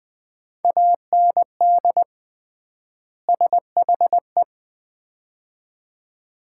Morse code, heard as a keyed tone, about 20 wpm, 700 Hz. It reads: AND SHE
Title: 3x Word-Spacing - Sets of 2 Words - 20wpm